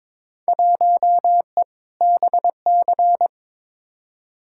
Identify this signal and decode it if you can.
Morse code, heard as a keyed tone, about 22 wpm, 700 Hz. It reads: 1E BC
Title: Field Day Exchanges - 22wpm